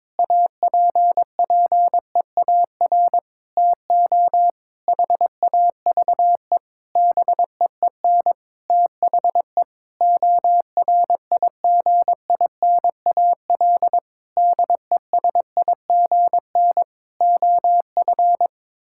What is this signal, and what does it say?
Morse code, heard as a keyed tone, about 22 wpm, 700 Hz. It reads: APPEAR TO HAVE BEEN THE ORIGINAL DESIGN OF
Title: Sentences from Top 1000 Words - Review Part 6 - 22wpm